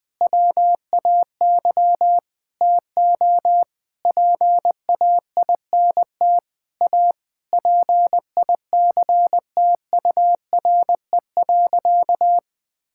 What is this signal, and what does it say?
Morse code, heard as a keyed tone, about 20 wpm, 700 Hz. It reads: WAY TO PAINT A PICTURE.
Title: Sentences with 8-Word Phrases - 20wpm